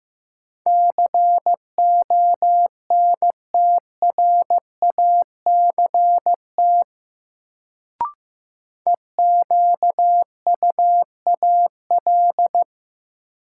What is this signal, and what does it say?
Morse code, heard as a keyed tone, about 15 wpm, 700 Hz. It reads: CONTRACT EQUAL
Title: Sentences from Top 1400 Words - New Word Drill - 15wpm